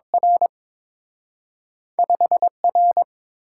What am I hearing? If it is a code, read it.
Morse code, heard as a keyed tone, about 30 wpm, 700 Hz. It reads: R 5R